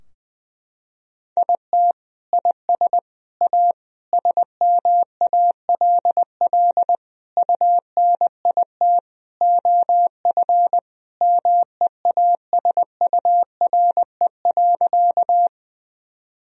Morse code, heard as a keyed tone, about 20 wpm, 700 Hz. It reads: IT IS A SMALL UNIT OF MEASURE.
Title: Sentences from Top 1000 Words - Review Part 2 - 20wpm